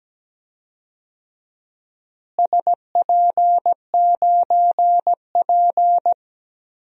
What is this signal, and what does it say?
Morse code, heard as a keyed tone, about 17 wpm, 700 Hz. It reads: SP9P
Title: DXCC-100 Call Signs - 4 Characters - Repeated 2x - 17wpm